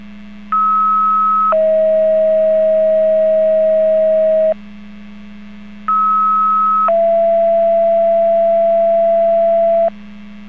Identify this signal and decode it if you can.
Two-tone page, heard as a one-second tone, then a three-second tone, about 1290 then 640 Hz, about 1290 then 670 Hz. It is heard from 0.5 to 9.9 s.